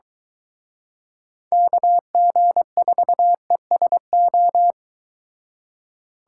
Morse code, heard as a keyed tone, about 23 wpm, 700 Hz. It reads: KG4ESO